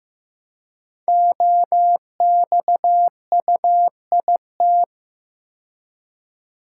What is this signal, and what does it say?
Morse code, heard as a keyed tone, about 15 wpm, 700 Hz. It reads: OXUIT
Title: Code Groups - 5 Letters - 15wpm